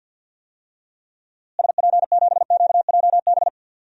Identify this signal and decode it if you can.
Morse code, heard as a keyed tone, about 52 wpm, 700 Hz. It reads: SP7QJB
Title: DXCC-100 Call Signs - 6 Characters - Repeated 3x - 50wpm